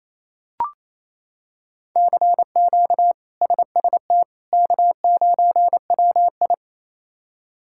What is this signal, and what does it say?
Morse code, heard as a keyed tone, about 28 wpm, 700 Hz. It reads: CQ SST K9WI
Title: K1USN SST QSO Practice - 28wpm